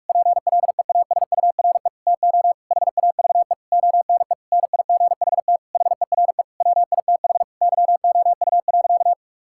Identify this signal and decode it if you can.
Morse code, heard as a keyed tone, about 46 wpm, 700 Hz. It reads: PLEASURE TO HAVE ONE NIGHT HERE WITH YOU.
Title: Sentences from Top 1600 Words 45wpm